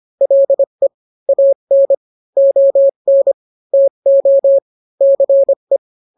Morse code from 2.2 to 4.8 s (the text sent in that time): ON TO